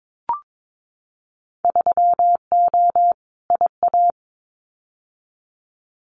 Morse code, heard as a keyed tone, about 22 wpm, 700 Hz. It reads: 3O IA
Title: Winter Field Day Exchanges 22wpm